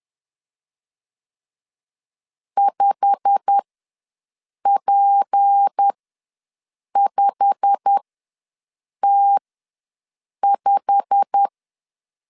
Morse code, heard as a keyed tone, about 10 wpm, spaced as for 6 wpm, 800 Hz. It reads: 5P5T5
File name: LEZIONI / 8 YJ45 / 8 YJ45 ws 10wpm 3843